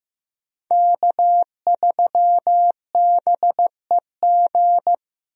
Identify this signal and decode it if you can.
Morse code, heard as a keyed tone, about 15 wpm, 700 Hz. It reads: K3BEG
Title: USA Call Signs - 5 Characters - Repeated 2x - 15wpm